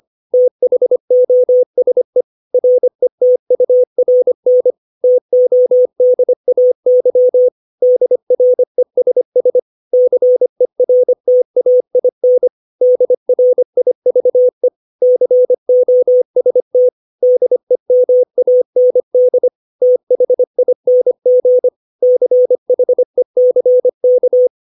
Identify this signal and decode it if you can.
Morse code, heard as a keyed tone, about 25 wpm, 500 Hz. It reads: THOSE RETURN TODAY DRESS CERTAIN DRIVE COST DEMAND THING CHECK
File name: 632 words / 632 Words 25WPM